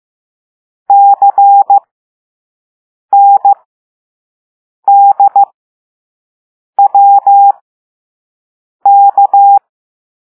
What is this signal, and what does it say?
Morse code, heard as a keyed tone, about 15 wpm, spaced as for 5 wpm, 800 Hz. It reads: CNDWK
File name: Morse / Level6-All-A